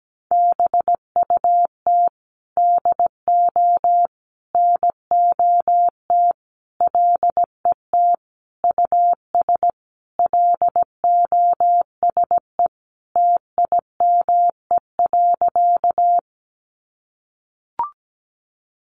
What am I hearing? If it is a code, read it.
Morse code, heard as a keyed tone, about 17 wpm, 700 Hz. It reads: BUT DO NOT LET US LOSE TIME.
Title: Sentences from Top 1100 Words 17wpm